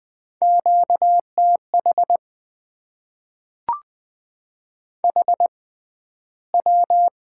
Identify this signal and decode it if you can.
Morse code, heard as a keyed tone, about 20 wpm, spaced as for 12 wpm, 700 Hz. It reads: QTH HW
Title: Top 20 QSO Elements - 6x Character Spacing - 20wpm